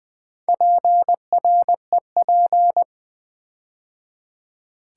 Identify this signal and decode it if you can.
Morse code, heard as a keyed tone, about 20 wpm, 700 Hz. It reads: PREP
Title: Build A Word - 4 Letters At A Time - 11-Letter Words - 20wpm